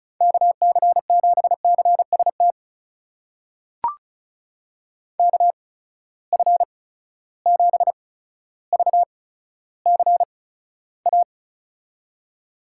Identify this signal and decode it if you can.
Morse code, heard as a keyed tone, about 36 wpm, spaced as for 15 wpm, 700 Hz. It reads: KC7CST KF7VCA
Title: USA Call Signs - 6 Characters - 8x Character Spacing - 35wpm